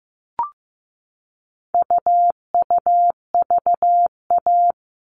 Morse code, heard as a keyed tone, about 15 wpm, 700 Hz. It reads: UUVA
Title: All Cut Numbers - 4-Digit 15wpm